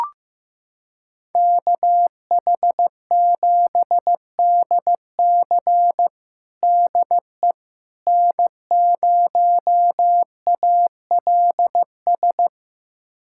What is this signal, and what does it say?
Morse code, heard as a keyed tone, about 15 wpm, 700 Hz. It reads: KH7DC DE N0ALS
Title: USA Callsign DE Callsign - 5 Characters - 15wpm